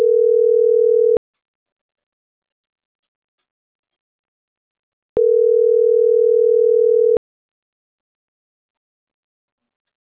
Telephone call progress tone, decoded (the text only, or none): ringback tone